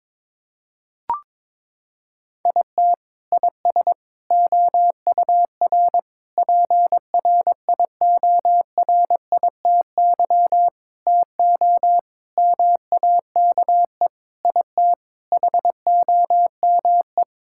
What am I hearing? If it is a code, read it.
Morse code, heard as a keyed tone, about 22 wpm, 700 Hz. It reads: IT IS OUR PRIORITY TO MAKE IT HOME
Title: Sentences from Top 1600 Words 22wpm